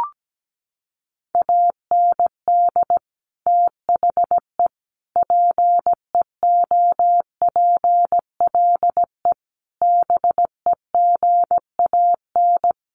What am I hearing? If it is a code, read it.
Morse code, heard as a keyed tone, about 17 wpm, 700 Hz. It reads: AND THE PEOPLE BEGAN
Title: Sentences from Top 1500 Words 17wpm